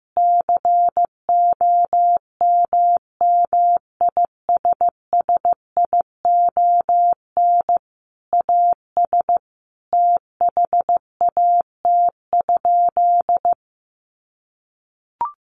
Morse code, heard as a keyed tone, about 15 wpm, 700 Hz. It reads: COMMISSION AS THAT?
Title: Sentences from Top 1400 Words 15wpm